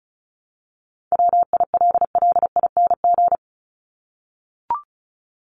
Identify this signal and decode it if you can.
Morse code, heard as a keyed tone, about 34 wpm, 700 Hz. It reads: WILLING
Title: Top 751-1000 Words 35wpm